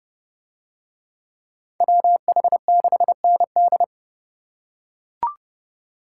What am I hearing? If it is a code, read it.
Morse code, heard as a keyed tone, about 27 wpm, 700 Hz. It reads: WH6ND